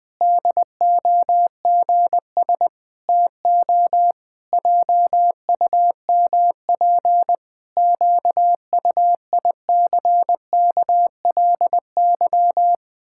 Morse code, heard as a keyed tone, about 19 wpm, 700 Hz. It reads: DOGS TO JUMP QUICKLY
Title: Pangram Challenge v2 - 20wpm